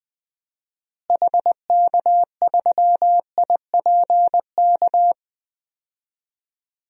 Morse code, heard as a keyed tone, about 20 wpm, 700 Hz. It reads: HK3IPK
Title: DXCC-100 Call Signs - 6 Characters - Repeated 3x - 20wpm